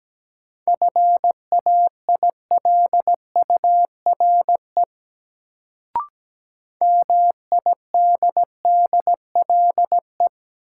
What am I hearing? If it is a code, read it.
Morse code, heard as a keyed tone, about 17 wpm, 700 Hz. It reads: FAILURE MIDDLE
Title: Sentences from Top 900 Words - New Word Drill - 17wpm